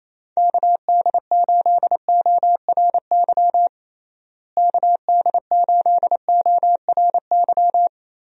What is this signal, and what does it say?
Morse code, heard as a keyed tone, about 28 wpm, 700 Hz. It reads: KD8ORY KD8ORY
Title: USA Call Signs - 6 Characters - Repeated 2x - 28wpm